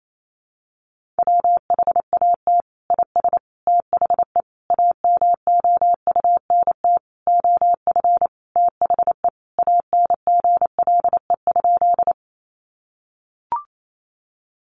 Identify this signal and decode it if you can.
Morse code, heard as a keyed tone, about 28 wpm, 700 Hz. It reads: WHAT IS THE AMOUNT OF THE ANGLE?
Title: Sentences from Top 1500 Words 28wpm